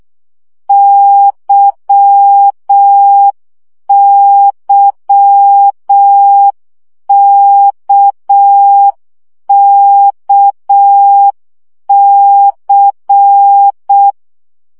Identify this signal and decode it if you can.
Morse code, heard as a keyed tone, about 6 wpm, 800 Hz. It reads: YYKKC
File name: CW gyakorlas k c y 5 baud